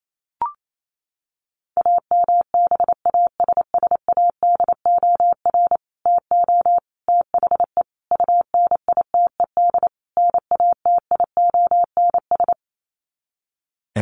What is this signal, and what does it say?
Morse code, heard as a keyed tone, about 28 wpm, 700 Hz. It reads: AMBASSADOR TO THE UNITED NATIONS
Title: Sets of 5 Words Encore2 28wpm